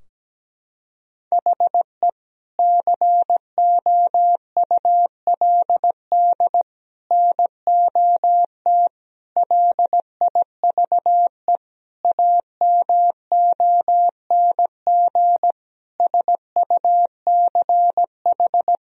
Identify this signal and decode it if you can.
Morse code, heard as a keyed tone, about 17 wpm, 700 Hz. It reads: HE COULD NOT LIVE AMONG SUCH
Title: Sentences from Top 1000 Words - Review Part 3 - 17wpm